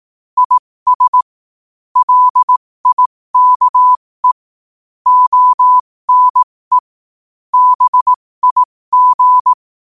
Morse code, heard as a keyed tone, about 18 wpm, spaced as for 15 wpm, 1000 Hz. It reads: IS LIKE ONE BIG
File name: qotdfiles / 15wpm